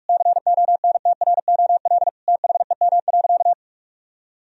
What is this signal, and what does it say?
Morse code, heard as a keyed tone, about 46 wpm, 700 Hz. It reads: CONTROL THEM.
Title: Sentences from Top 1000 Words - Review Part 2 - 45wpm